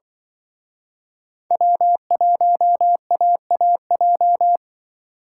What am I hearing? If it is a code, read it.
Morse code, heard as a keyed tone, about 24 wpm, 700 Hz. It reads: W1AAJ
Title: Speed Racing - USA 1x3 Call Signs 20wpm